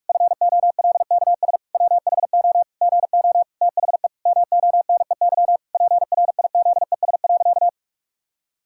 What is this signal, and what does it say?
Morse code, heard as a keyed tone, about 46 wpm, 700 Hz. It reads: FOLKS WHO GO THE MONEY PRIZES.
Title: Pangram Challenge v2 - 45wpm